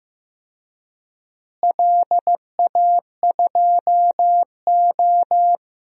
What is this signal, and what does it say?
Morse code, heard as a keyed tone, about 15 wpm, 700 Hz. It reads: LA2O
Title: DXCC-100 Call Signs - 4 Characters - Repeated 3x - 15wpm